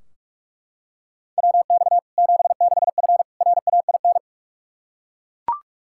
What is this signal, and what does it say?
Morse code, heard as a keyed tone, about 42 wpm, 700 Hz. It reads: WX 76F RAIN
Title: Practice QSO - Exchange 2 - Info - 45wpm